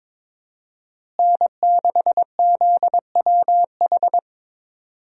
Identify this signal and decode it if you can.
Morse code, heard as a keyed tone, about 22 wpm, 700 Hz. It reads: N6ZWH
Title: USA Call Signs - 5 Characters - Repeated 2x - 22wpm